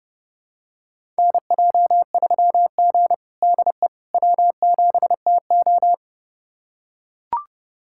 Morse code, heard as a keyed tone, about 30 wpm, 700 Hz. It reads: NJ3G DE W7TO
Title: USA Callsign DE Callsign - 4 Characters - 30wpm